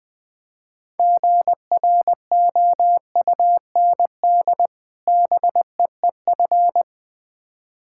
Morse code, heard as a keyed tone, about 20 wpm, 700 Hz. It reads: GROUND BEEF
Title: Difficult Words with Hints - 20wpm